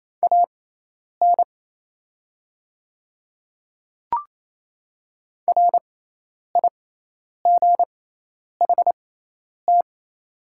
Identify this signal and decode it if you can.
Morse code, heard as a keyed tone, about 28 wpm, spaced as for 10 wpm, 700 Hz. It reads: AN RIGHT
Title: 6x Character Spacing - Top 300 Words - 28wpm